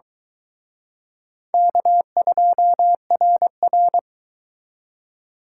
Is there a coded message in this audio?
Yes: Morse code from 1.5 to 4.0 s, heard as a keyed tone, about 24 wpm, 700 Hz. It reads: K2RR